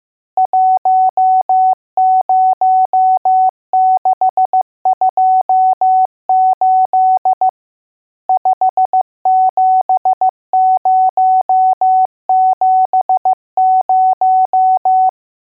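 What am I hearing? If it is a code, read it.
Morse code, heard as a keyed tone, about 15 wpm, 750 Hz. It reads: 10628 57070
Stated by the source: Morse code lesson: test on all digits.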